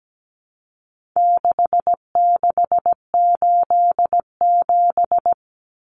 Morse code, heard as a keyed tone, about 17 wpm, 700 Hz. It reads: 6687
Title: Common Cut Numbers - 4-Digit 17wpm